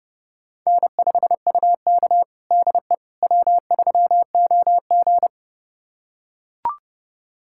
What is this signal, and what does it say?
Morse code, heard as a keyed tone, about 30 wpm, 700 Hz. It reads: N5UK DE W3OG
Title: USA Callsign DE Callsign - 4 Characters - 30wpm